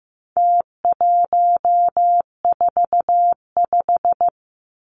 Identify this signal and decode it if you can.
Morse code, heard as a keyed tone, about 15 wpm, 700 Hz. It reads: T145
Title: Numbers - Contest Style - 4-Digit 15wpm